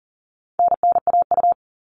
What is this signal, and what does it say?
Morse code, heard as a keyed tone, about 40 wpm, 700 Hz. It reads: NNAU
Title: All Cut Numbers - 4-Digit 40wpm